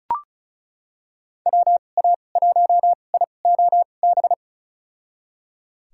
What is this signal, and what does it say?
Morse code, heard as a keyed tone, about 36 wpm, spaced as for 25 wpm, 700 Hz. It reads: WA1IOB
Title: USA Call Signs - 6 Characters - 2x Character Spacing - 35wpm